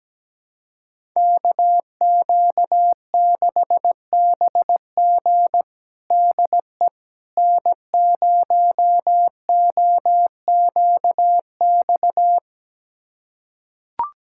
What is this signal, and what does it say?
Morse code, heard as a keyed tone, about 17 wpm, 700 Hz. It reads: KQ6BG DE N0OQX
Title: USA Callsign DE Callsign - 5 Characters - 17wpm